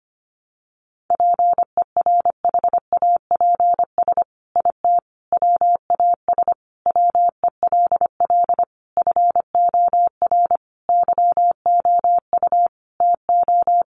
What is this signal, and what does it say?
Morse code, heard as a keyed tone, about 25 wpm, 700 Hz. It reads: PERHAPS IT WAS WELL FOR YOU TO
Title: Sentences from Top 1400 Words 25wpm